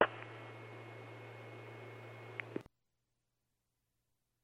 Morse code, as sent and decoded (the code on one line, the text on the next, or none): none